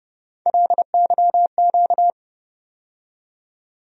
Morse code, heard as a keyed tone, about 30 wpm, 700 Hz. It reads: LYQ
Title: USA Call Sign Postfixes - 3 Letters 30wpm